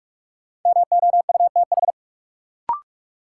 Morse code, heard as a keyed tone, about 46 wpm, 700 Hz. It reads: MOUTH
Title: Sentences from Top 1500 Words - New Word Drill Alt - 45wpm